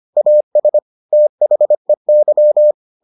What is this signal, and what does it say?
Morse code, heard as a keyed tone, about 25 wpm, 600 Hz. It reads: AS THEY